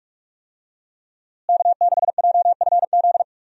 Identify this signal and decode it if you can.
Morse code, heard as a keyed tone, about 46 wpm, 700 Hz. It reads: K6JFZ
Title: USA Call Signs - 5 Characters - Repeated 2x - 45wpm